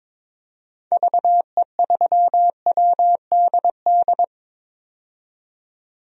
Morse code, heard as a keyed tone, about 22 wpm, 700 Hz. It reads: VE3WDD
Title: DXCC-100 Call Signs - 6 Characters - 22wpm